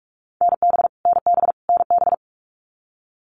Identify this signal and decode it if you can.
Morse code, heard as a keyed tone, about 44 wpm, 700 Hz. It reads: NB NB NB
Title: Canadian Provinces and Territories - Repeated 3x - 45wpm